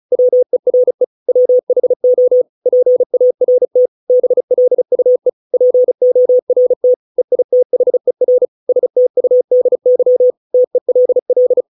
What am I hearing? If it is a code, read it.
Morse code, heard as a keyed tone, about 34 wpm, 500 Hz. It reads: WERE WHO PART BLUE PORT EITHER STUDY TELL